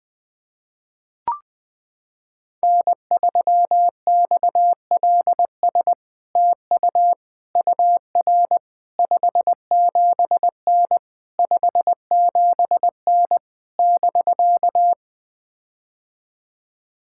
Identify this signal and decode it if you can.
Morse code, heard as a keyed tone, about 21 wpm, 700 Hz. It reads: N3XLS TU UR 57N 57N <BK>
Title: POTA QSO Practice - 20wpm